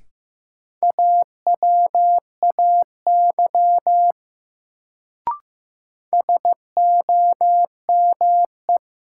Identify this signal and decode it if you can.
Morse code, heard as a keyed tone, about 15 wpm, 700 Hz. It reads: AWAY SOME
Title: Sets of 2 Words - New Word Drill - 15wpm